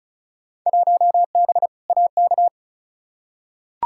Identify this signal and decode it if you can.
Morse code, heard as a keyed tone, about 36 wpm, 700 Hz. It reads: 1B AK
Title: Field Day Exchanges Encore - 35wpm